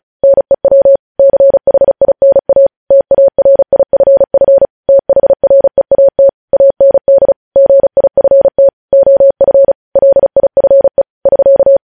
Morse code, heard as a keyed tone, about 36 wpm, 550 Hz. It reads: NEW CHINA TARIFF THREAT AND GIFT OF LIFE <SK>
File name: podcast / News headlines-1x-35x35